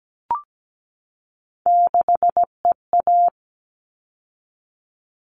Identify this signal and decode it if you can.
Morse code, heard as a keyed tone, about 17 wpm, 700 Hz. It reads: 6EA
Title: All Cut Numbers - 3-Digit 17wpm